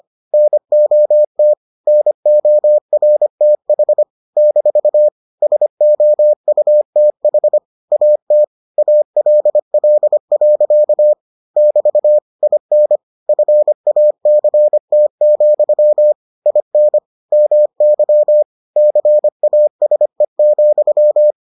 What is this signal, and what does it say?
Morse code, heard as a keyed tone, about 25 wpm, 600 Hz. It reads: NOT NORTH - SOUTH AT ALL. <BT> IN FACT, IN MY CASE,